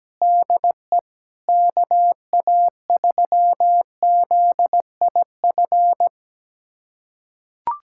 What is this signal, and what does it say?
Morse code, heard as a keyed tone, about 17 wpm, 700 Hz. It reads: DE KA3ZIF
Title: USA Callsign DE Callsign 17wpm